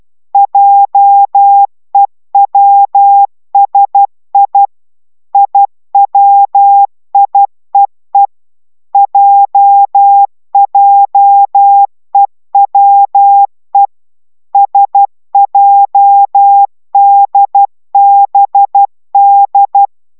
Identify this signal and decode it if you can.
Morse code, heard as a keyed tone, about 12 wpm, 800 Hz. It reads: JEWSI IWIEE JJEWE SJDBD